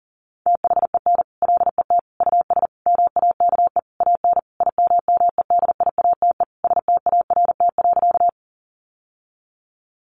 Morse code, heard as a keyed tone, about 40 wpm, 700 Hz. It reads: THEN LET US MAKE AN IMMEDIATE START.
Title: Sentences from Top 1200 Words 40wpm